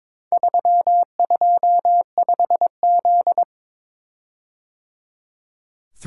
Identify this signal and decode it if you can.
Morse code, heard as a keyed tone, about 22 wpm, 700 Hz. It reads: 325Z
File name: Four Letter-Number Combinations - 22wpm